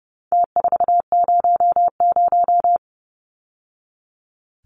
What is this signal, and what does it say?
Morse code, heard as a keyed tone, about 30 wpm, 700 Hz. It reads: T400
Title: Numbers - Contest Style - 4-Digit 30wpm